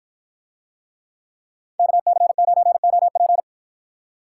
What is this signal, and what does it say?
Morse code, heard as a keyed tone, about 53 wpm, 700 Hz. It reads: KC9OP